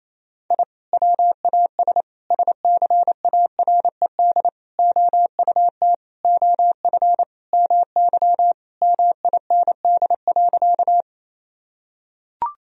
Morse code, heard as a keyed tone, about 28 wpm, 700 Hz. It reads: I WAS SCARED OUT OF MY MIND.
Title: Sentences from Top 1000 Words - Review Part 4 - 28wpm